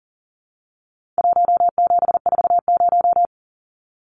Morse code, heard as a keyed tone, about 40 wpm, 700 Hz. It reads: 1740
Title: Numbers - 4-Digit 40wpm